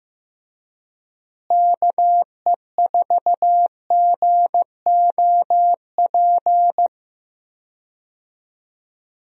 Morse code, heard as a keyed tone, about 15 wpm, 700 Hz. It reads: KE4GOP
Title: USA Call Signs - 6 Characters - Repeated 3x - 15wpm